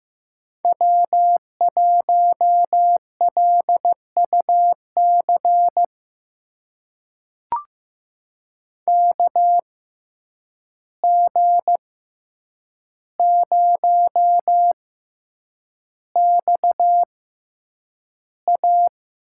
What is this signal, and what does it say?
Morse code, heard as a keyed tone, about 15 wpm, spaced as for 8 wpm, 700 Hz. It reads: W1LUC KG0XA
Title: USA Call Signs - 5 Characters - 6x Character Spacing - 15wpm